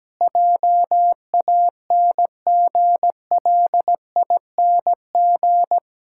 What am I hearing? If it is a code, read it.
Morse code, heard as a keyed tone, about 17 wpm, 700 Hz. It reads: JANGLING
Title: Pangram Challenge v2 - 17wpm